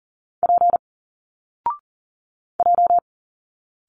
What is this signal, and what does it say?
Morse code, heard as a keyed tone, about 39 wpm, 700 Hz. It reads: P J
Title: Single Letters 40wpm